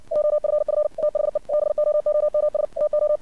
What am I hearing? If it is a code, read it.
Morse code, heard as a keyed tone, about 57 wpm, 600 Hz. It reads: OFF THE BOOMS TO
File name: archived / CRSnet